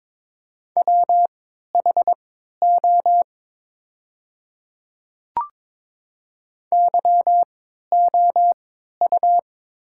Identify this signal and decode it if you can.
Morse code, heard as a keyed tone, about 22 wpm, spaced as for 13 wpm, 690 Hz. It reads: WHO YOU
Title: 3x Character Spacing - Top 100 Words - 22wpm